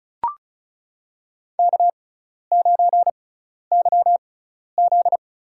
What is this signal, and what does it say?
Morse code, heard as a keyed tone, about 34 wpm, spaced as for 12 wpm, 700 Hz. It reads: K9YZ
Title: USA Call Signs - 4 Characters - 6x Character Spacing - 35wpm